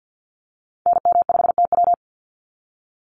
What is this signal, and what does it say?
Morse code, heard as a keyed tone, about 50 wpm, 700 Hz. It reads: NM5TW
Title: NA SOTA Activator Callsigns 50wpm